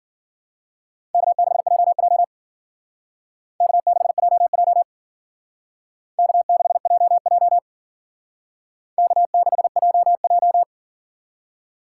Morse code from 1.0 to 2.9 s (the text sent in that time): K6JJ